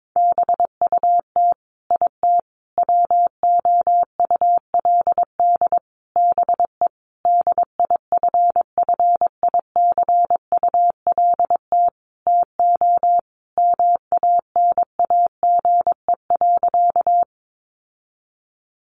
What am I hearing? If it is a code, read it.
Morse code, heard as a keyed tone, about 22 wpm, 700 Hz. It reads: BUT IT WOULD BE DIFFICULT TO MANAGE.
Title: Sentences from Top 1400 Words 22wpm